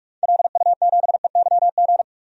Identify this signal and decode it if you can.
Morse code, heard as a keyed tone, about 46 wpm, 700 Hz. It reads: LU7EYG
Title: DXCC-100 Call Signs - 6 Characters - 45wpm